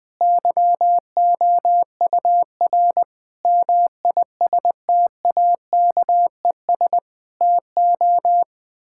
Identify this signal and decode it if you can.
Morse code, heard as a keyed tone, about 20 wpm, 700 Hz. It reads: YOUR MISTAKES TO
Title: Difficult Sentences with Hints - 20wpm